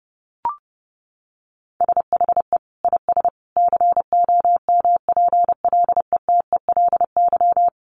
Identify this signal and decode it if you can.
Morse code, heard as a keyed tone, about 30 wpm, 700 Hz. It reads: SHE IS COMPLETELY
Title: Additional Easy Phrases 30wpm